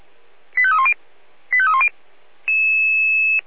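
Five-tone selective call (ZVEI), heard as quick runs of five tones, tones of about 70 ms. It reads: 85310 85310
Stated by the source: Five-tone ZVEI selective call received on an SDR.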